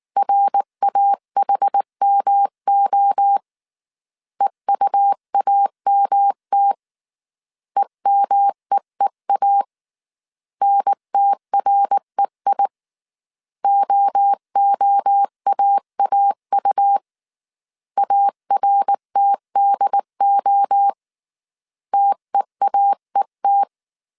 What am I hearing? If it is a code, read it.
Morse code, heard as a keyed tone, about 20 wpm, 800 Hz. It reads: RAHMO EUAMT EMEEA NTREI OOAAU ARTDO TEAET